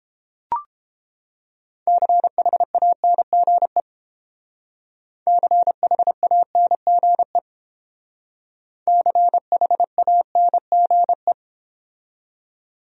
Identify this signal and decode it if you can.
Morse code, heard as a keyed tone, about 29 wpm, 700 Hz. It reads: CHANGE CHANGE CHANGE